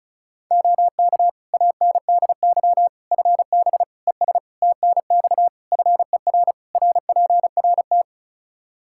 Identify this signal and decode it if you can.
Morse code, heard as a keyed tone, about 34 wpm, 700 Hz. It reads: OK ANDY FB ES TNX FER RPRT
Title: Practice QSO - Exchange 2 - Info - 35wpm